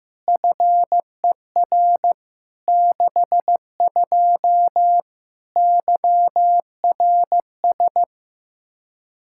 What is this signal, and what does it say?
Morse code, heard as a keyed tone, about 15 wpm, 700 Hz. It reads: FER 62 YRS
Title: K1USN SST QSO Practice 15wpm